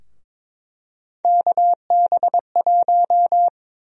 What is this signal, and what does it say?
Morse code, heard as a keyed tone, about 22 wpm, 700 Hz. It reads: KB1
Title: USA Call Sign Prefixes - 22wpm